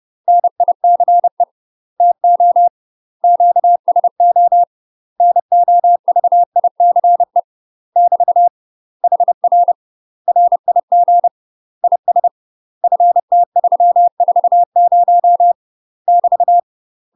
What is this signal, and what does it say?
Morse code, heard as a keyed tone, about 30 wpm, 700 Hz. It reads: NICE TO QSO NOVICE <BT> HR RIG IS FT340 <BT>
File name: UK MASTER QSOs TRAINING/30WPM